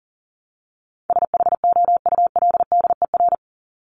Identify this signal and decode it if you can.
Morse code, heard as a keyed tone, about 40 wpm, 700 Hz. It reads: SHOULDER
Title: Top 1501-1750 Words 40wpm